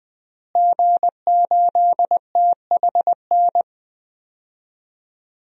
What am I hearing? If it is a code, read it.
Morse code, heard as a keyed tone, about 20 wpm, 700 Hz. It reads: G8THN
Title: DXCC-100 Call Signs - 5 Characters - Repeated 2x - 20wpm